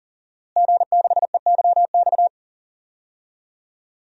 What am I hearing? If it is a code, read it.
Morse code, heard as a keyed tone, about 40 wpm, 700 Hz. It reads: G6EYX
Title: DXCC-100 Call Signs - 5 Characters - Repeated 3x - 40wpm